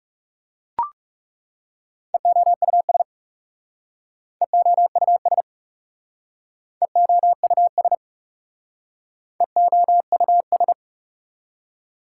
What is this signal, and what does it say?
Morse code, heard as a keyed tone, about 38 wpm, 700 Hz. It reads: EOUS EOUS EOUS EOUS
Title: Speed-Racing Prefixes and Suffixes 30wpm